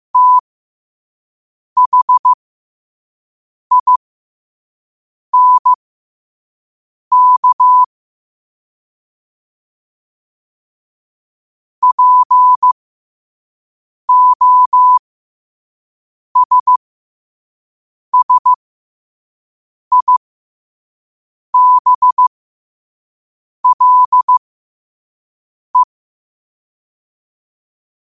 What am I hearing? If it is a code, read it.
Morse code, heard as a keyed tone, about 15 wpm, spaced as for 5 wpm, 1000 Hz. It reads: THINK POSSIBLE
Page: qotd-at-5-wpm